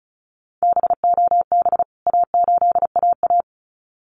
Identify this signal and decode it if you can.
Morse code, heard as a keyed tone, about 34 wpm, 700 Hz. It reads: BOB A8AA
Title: CWT Exchange 35wpm